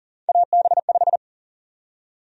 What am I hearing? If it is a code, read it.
Morse code, heard as a keyed tone, about 40 wpm, 700 Hz. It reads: AB5
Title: USA Call Sign Prefixes - 40wpm